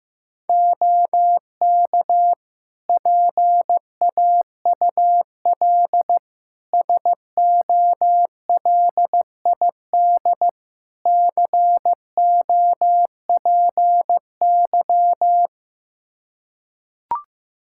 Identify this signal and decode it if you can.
Morse code, heard as a keyed tone, about 15 wpm, 700 Hz. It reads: OK PAUL SOLID COPY
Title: K1USN SST QSO Practice 15wpm